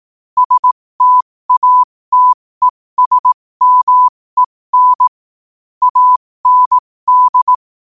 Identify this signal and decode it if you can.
Morse code, heard as a keyed tone, about 18 wpm, spaced as for 15 wpm, 1000 Hz. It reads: STATESMEN AND
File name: qotdfiles / 15wpm